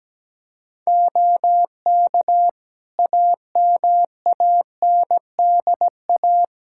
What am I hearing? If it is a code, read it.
Morse code, heard as a keyed tone, about 17 wpm, 700 Hz. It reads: OK AMANDA
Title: Practice QSO - Exchange 4 - Ending - 17wpm